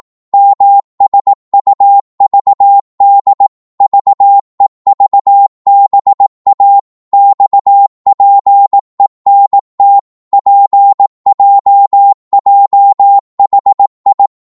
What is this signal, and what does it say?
Morse code, heard as a keyed tone, about 18 wpm, 800 Hz. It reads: MSUVD VEVBA XPENT PJJHI